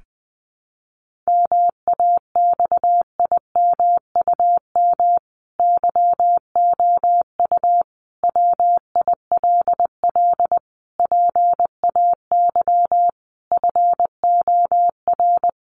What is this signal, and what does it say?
Morse code, heard as a keyed tone, about 20 wpm, 700 Hz. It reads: MAXIMUM YOU WILL PAY FOR
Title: Sentences from Top 1300 Words 20wpm